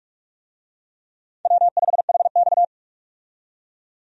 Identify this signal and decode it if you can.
Morse code, heard as a keyed tone, about 46 wpm, 700 Hz. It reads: W5HX
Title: USA Call Signs - 4 Characters - 45wpm